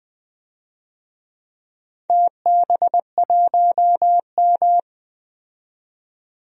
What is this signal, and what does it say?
Morse code, heard as a keyed tone, about 20 wpm, 700 Hz. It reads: TB1M
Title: DXCC-100 Call Signs - 4 Characters - 20wpm